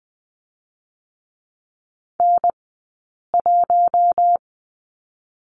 Morse code, heard as a keyed tone, about 20 wpm, 700 Hz. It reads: N 1
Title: 040 Lesson 8d - Number 1 - Two Characters 20wpm